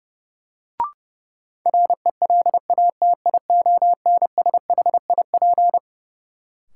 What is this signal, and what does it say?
Morse code, heard as a keyed tone, about 32 wpm, 700 Hz. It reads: RELATIONSHIP
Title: Sentences from Top 1100 Words - New Word Drill Alt - 30wpm